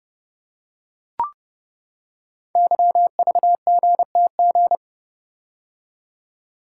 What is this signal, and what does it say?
Morse code, heard as a keyed tone, about 30 wpm, 700 Hz. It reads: YVGTG
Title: Code Groups - 5 Letters - 30wpm